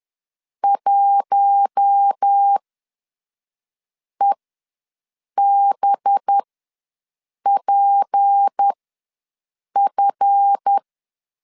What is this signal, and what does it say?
Morse code, heard as a keyed tone, about 10 wpm, spaced as for 5 wpm, 800 Hz. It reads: 1EBPF
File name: LEZIONI / 8 YJ45 / 8 YJ4 ws 10wpm 4020